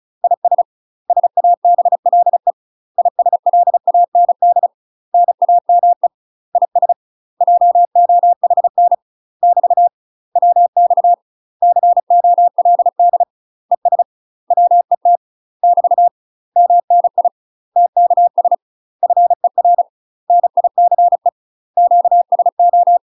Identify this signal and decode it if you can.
Morse code, heard as a keyed tone, about 34 wpm, 700 Hz. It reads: IS SABLE ISLAND NAME IS JOHN <BT> WX COLD ES WET <BT> MNI TKS FER NICE QSO
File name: UK MASTER QSOs TRAINING / 35WPM / UK MASTER QSOs 0040